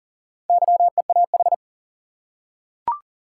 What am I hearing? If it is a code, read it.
Morse code, heard as a keyed tone, about 39 wpm, 700 Hz. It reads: YEAH
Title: Four Letter Words - 40wpm